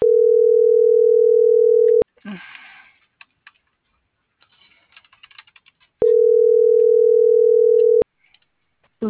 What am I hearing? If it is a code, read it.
Ringback tone (North America), 440 + 480 Hz.